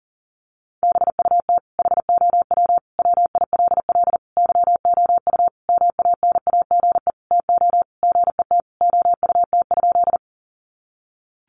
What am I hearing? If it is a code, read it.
Morse code, heard as a keyed tone, about 40 wpm, 700 Hz. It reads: BUT HOW WILL YOU MANAGE TO GET OUT?
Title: Sentences from Top 1400 Words 40wpm